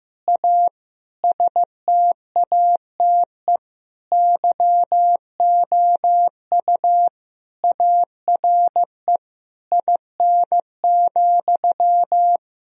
Morse code, heard as a keyed tone, about 16 wpm, 700 Hz. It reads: A STATE YOU ARE IN,
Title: Sentences from Top 1000 Words - Review Part 2 - 15wpm